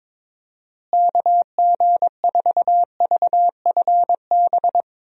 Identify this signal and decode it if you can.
Morse code, heard as a keyed tone, about 22 wpm, 700 Hz. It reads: KG4VFB
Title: USA Call Signs - 6 Characters - Repeated 2x - 22wpm